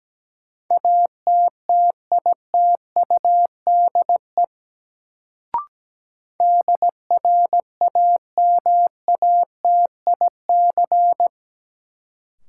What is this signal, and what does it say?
Morse code, heard as a keyed tone, about 17 wpm, 700 Hz. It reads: ATTITUDE DRAMATIC
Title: Sentences from Top 1300 Words - New Word Drill Alt - 17wpm